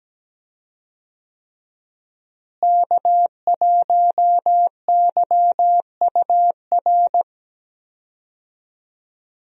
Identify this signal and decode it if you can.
Morse code, heard as a keyed tone, about 17 wpm, 700 Hz. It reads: K1YUR
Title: USA Call Signs - 5 Characters - Repeated 3x - 17wpm